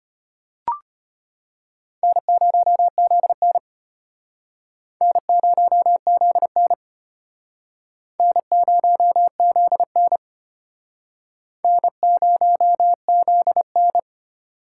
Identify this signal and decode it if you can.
Morse code, heard as a keyed tone, about 32 wpm, 700 Hz. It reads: N0ZN N0ZN N0ZN N0ZN